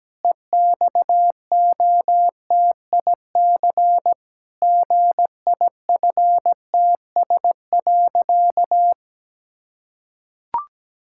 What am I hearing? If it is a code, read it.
Morse code, heard as a keyed tone, about 17 wpm, 700 Hz. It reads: EXOTIC GIFTS.
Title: Pangram Challenge v2 - 17wpm